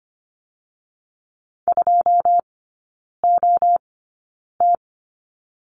Morse code, heard as a keyed tone, about 25 wpm, spaced as for 9 wpm, 690 Hz. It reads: 2OT
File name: Lesson 14e - Number 2 - Three Characters 25wpm